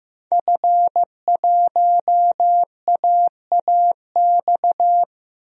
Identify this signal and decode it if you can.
Morse code, heard as a keyed tone, about 16 wpm, 700 Hz. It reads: F1AAX
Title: DXCC-100 Call Signs - 5 Characters - 15wpm